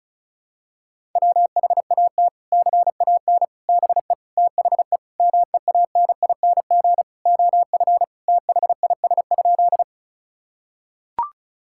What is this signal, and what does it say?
Morse code, heard as a keyed tone, about 36 wpm, 700 Hz. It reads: WHAT CAN BE THE MEANING OF THIS?
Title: Sentences from Top 1000 Words - Review Part 6 - 35wpm